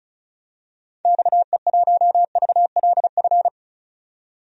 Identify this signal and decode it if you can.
Morse code, heard as a keyed tone, about 34 wpm, 700 Hz. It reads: XE1VLF
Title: DXCC-100 Call Signs - 6 Characters - 35wpm